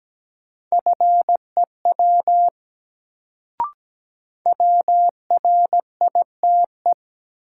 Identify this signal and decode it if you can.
Morse code, heard as a keyed tone, about 17 wpm, 700 Hz. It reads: FEW WRITE
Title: Sentences from Top 300 Words - New Word Drill - 17wpm